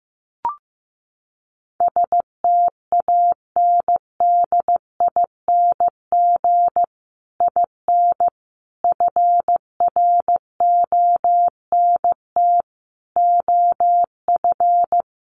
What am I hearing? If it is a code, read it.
Morse code, heard as a keyed tone, about 15 wpm, 700 Hz. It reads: STANDING IN FRONT OF
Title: Sets of 5 Words Encore2 15wpm